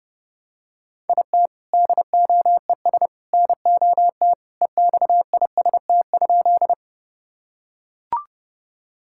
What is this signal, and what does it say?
Morse code, heard as a keyed tone, about 30 wpm, 700 Hz. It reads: IT DOES NOT EXIST?
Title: Sentences from Top 1000 Words - Review Part 5 - 30wpm